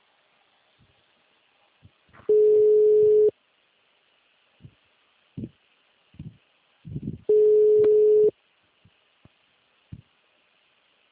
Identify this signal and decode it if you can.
Ringback tone (Europe, 425 Hz).
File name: out- -2043- - - .4